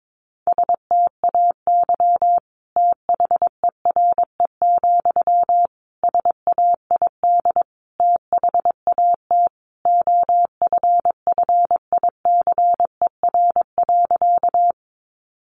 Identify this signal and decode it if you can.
Morse code, heard as a keyed tone, about 22 wpm, 700 Hz. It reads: STAY THERE, SAID THAT OFFICER.
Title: Sentences from Top 1200 Words 22wpm